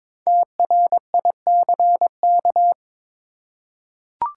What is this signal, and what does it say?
Morse code, heard as a keyed tone, about 22 wpm, 700 Hz. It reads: TRICK
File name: 5 Letter Words - First 3 Letters Given - 22wpm